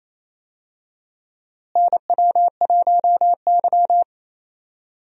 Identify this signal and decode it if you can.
Morse code, heard as a keyed tone, about 28 wpm, 700 Hz. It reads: NW1Y
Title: USA Call Signs - 4 Characters - Repeated 2x - 28wpm